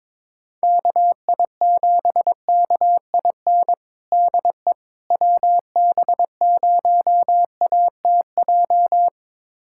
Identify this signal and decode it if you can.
Morse code, heard as a keyed tone, about 21 wpm, 700 Hz. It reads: KI7KIN DE WB0ATJ
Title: USA Callsign DE Callsign 22wpm